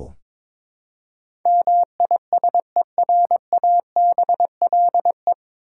Morse code, heard as a keyed tone, about 22 wpm, 700 Hz. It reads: MISERABLE
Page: Morse Code Ninja practice 22wpm